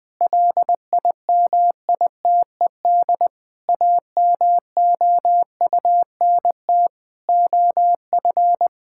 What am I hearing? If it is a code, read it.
Morse code, heard as a keyed tone, about 21 wpm, 700 Hz. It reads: LIMITED AMOUNT OF
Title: Sentences from Top 1000 Words - Review Part 6 - 20wpm